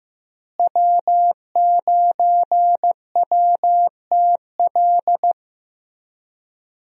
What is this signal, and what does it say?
Morse code, heard as a keyed tone, about 15 wpm, 700 Hz. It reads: W9WTL
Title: USA Call Signs - 5 Characters - Repeated 2x - 15wpm